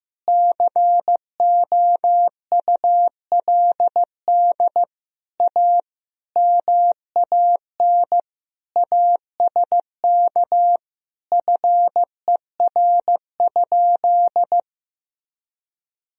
Morse code, heard as a keyed tone, about 15 wpm, 700 Hz. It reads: COULD A MAN ASK FER?
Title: Sentences from Top 200 Words and Abbreviations 15wpm